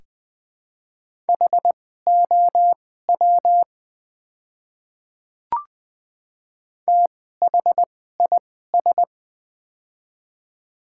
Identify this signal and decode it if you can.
Morse code, heard as a keyed tone, about 20 wpm, spaced as for 14 wpm, 700 Hz. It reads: HOW THIS
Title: 2x Character Spacing - Top 300 Words - 20wpm